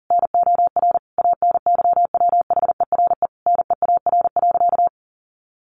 Morse code, heard as a keyed tone, about 40 wpm, 700 Hz. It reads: NOR ANYWHERE NEAR.
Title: Sentences from Top 1500 Words 40wpm